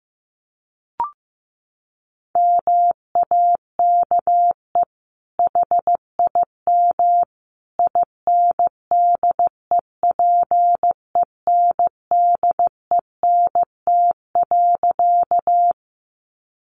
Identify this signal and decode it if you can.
Morse code, heard as a keyed tone, about 15 wpm, 700 Hz. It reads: MAKE HIM INDEPENDENT.
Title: Sentences from Top 900 Words 15wpm